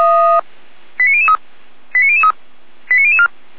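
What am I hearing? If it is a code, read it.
Five-tone selective call (ZVEI), heard as quick runs of five tones, tones of about 80 ms. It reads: 89003 89003 89004
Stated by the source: ZVEI five-tone call 89011 received by SDR.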